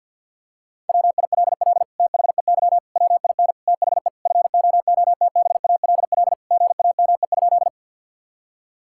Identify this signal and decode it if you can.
Morse code, heard as a keyed tone, about 50 wpm, 700 Hz. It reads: WILL THEY WIN THE FOOTBALL GAME?
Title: Sentences from Top 1600 Words 50wpm